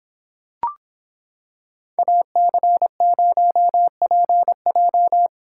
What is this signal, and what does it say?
Morse code, heard as a keyed tone, about 26 wpm, 700 Hz. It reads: AC0PJ